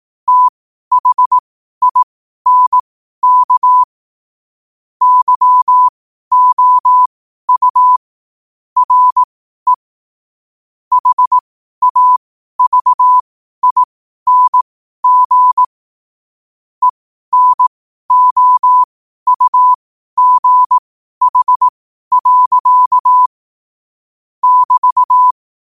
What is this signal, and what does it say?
Morse code, heard as a keyed tone, about 18 wpm, spaced as for 12 wpm, 1000 Hz. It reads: THINK YOU RE HAVING ENOUGH. <BT>
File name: 12wpm